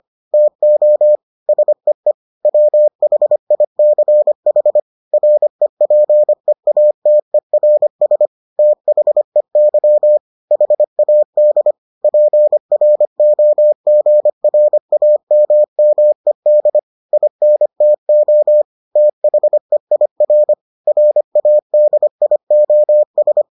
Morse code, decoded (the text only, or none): TO SEE WHICH REPEATERS THEY HAD PROGRAMMED INTO THEIR RADIOS